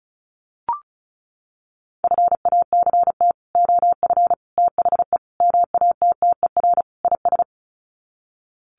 Morse code, heard as a keyed tone, about 34 wpm, 700 Hz. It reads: FACT OF THE MATTER IS
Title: Sets of 5 Words 35wpm